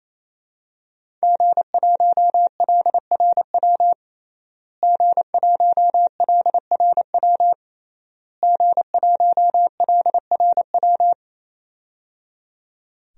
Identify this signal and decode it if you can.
Morse code, heard as a keyed tone, about 28 wpm, 700 Hz. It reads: G1LRW G1LRW G1LRW
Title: DXCC-100 Call Signs - 5 Characters - Repeated 3x - 28wpm